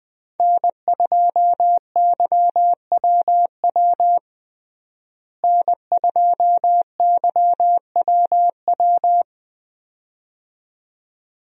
Morse code, heard as a keyed tone, about 21 wpm, 700 Hz. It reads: N2YWW N2YWW
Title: USA Call Signs - 5 Characters - Repeated 2x - 20wpm